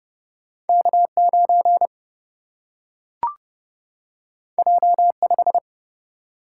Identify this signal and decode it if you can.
Morse code, heard as a keyed tone, about 30 wpm, 700 Hz. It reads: K9 J5
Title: Single-Letter Single-Number 30wpm